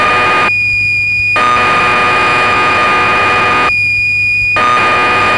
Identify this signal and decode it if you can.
Cassette data (Kansas City type).
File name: magazines / Acorn User / 1984-09 BBC Programs